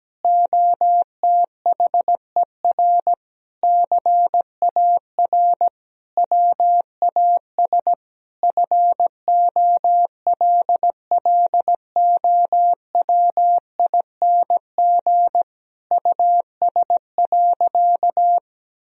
Morse code, heard as a keyed tone, about 17 wpm, 700 Hz. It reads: OTHER CAR WAS FOLLOWING US.
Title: Sentences from Top 1000 Words - Review Part 4 - 17wpm